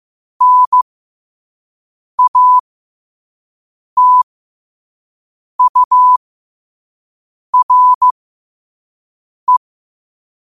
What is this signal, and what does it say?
Morse code, heard as a keyed tone, about 15 wpm, spaced as for 5 wpm, 1000 Hz. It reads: NATURE
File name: qotdfiles / 5wpm